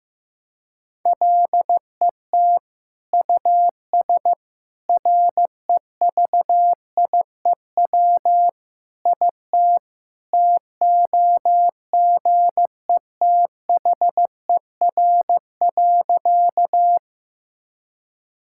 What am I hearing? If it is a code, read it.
Morse code, heard as a keyed tone, about 16 wpm, 700 Hz. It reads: LET US REVIEW IT TOGETHER.
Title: Sentences from Top 1000 Words - Review Part 6 - 15wpm